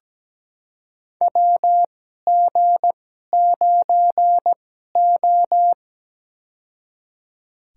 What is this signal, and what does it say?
Morse code, heard as a keyed tone, about 17 wpm, spaced as for 12 wpm, 700 Hz. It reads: WG9O
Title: USA Call Signs - 4 Characters - 2x Character Spacing - 17wpm